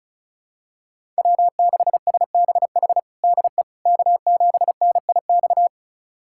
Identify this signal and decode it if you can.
Morse code, heard as a keyed tone, about 34 wpm, 700 Hz. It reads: W6SBH DE K7NIX
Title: USA Callsign DE Callsign 35wpm